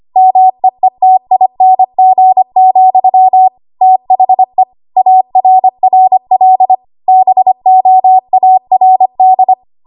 Morse code, heard as a keyed tone, about 25 wpm, 750 Hz. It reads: MEETING, THE ARRL BOARD
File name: Archive / 25 WPM / 25WPM